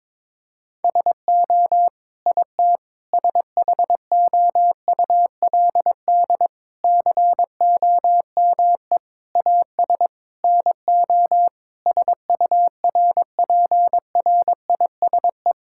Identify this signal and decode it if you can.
Morse code, heard as a keyed tone, about 22 wpm, 700 Hz. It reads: SO IT SHOULD COME AS NO SURPRISE
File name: practice / 22wpm / Sets of 8 Words - 22wpm